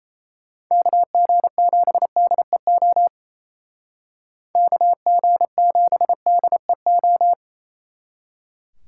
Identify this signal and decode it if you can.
Morse code, heard as a keyed tone, about 35 wpm, 700 Hz. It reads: KG7DEO KG7DEO
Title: Speed Racing - USA 2x3 Call Signs 28wpm